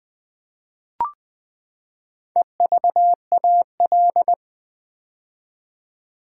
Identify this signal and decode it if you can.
Morse code, heard as a keyed tone, about 20 wpm, 700 Hz. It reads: EVAL
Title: Build A Word - 4 Letters At A Time - 10-Letter Words - 20wpm